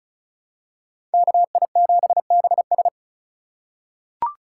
Morse code, heard as a keyed tone, about 34 wpm, 700 Hz. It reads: KI7BS
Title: USA Call Signs - 5 Characters - 3x Character Spacing - 35wpm